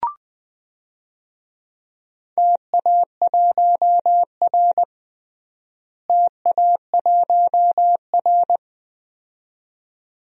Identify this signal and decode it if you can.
Morse code, heard as a keyed tone, about 20 wpm, 700 Hz. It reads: TA1R TA1R
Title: DXCC-100 Call Signs - 4 Characters - Repeated 2x - 20wpm